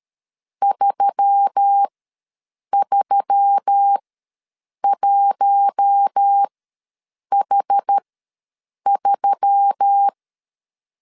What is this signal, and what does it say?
Morse code, heard as a keyed tone, about 13 wpm, spaced as for 7 wpm, 800 Hz. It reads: 331H3
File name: LEZIONI / 5 LBVC / 5 L ws 12wpm 2958